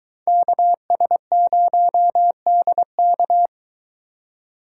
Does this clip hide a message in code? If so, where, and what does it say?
Yes: Morse code from 0.3 to 3.5 s, heard as a keyed tone, about 23 wpm, 700 Hz. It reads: KS0DK